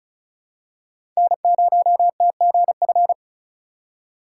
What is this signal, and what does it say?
Morse code, heard as a keyed tone, about 36 wpm, 700 Hz. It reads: N0TGF
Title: USA Call Signs - 5 Characters - 4x Character Spacing - 35wpm